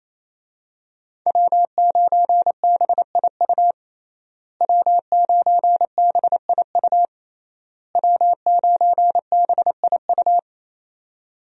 Morse code, heard as a keyed tone, about 28 wpm, 700 Hz. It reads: W9BIU W9BIU W9BIU
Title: USA Call Signs - 5 Characters - Repeated 3x - 28wpm